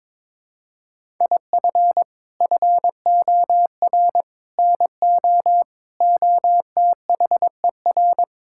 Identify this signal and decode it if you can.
Morse code, heard as a keyed tone, about 22 wpm, 700 Hz. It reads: IF FOR NO OTHER
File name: practice / 22wpm / Sets of 6 Words - 22wpm